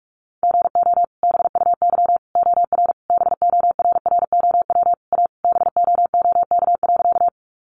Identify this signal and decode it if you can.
Morse code, heard as a keyed tone, about 44 wpm, 700 Hz. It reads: GO BUY OR BORROW A BOOK.
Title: Sentences from Top 1500 Words 45wpm